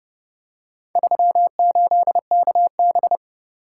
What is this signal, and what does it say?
Morse code, heard as a keyed tone, about 29 wpm, 700 Hz. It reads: 38KB